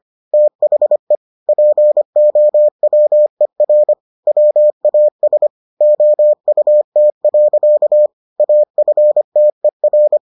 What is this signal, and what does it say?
Morse code, heard as a keyed tone, about 25 wpm, 600 Hz. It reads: THE POWER WAS OUT. AFTER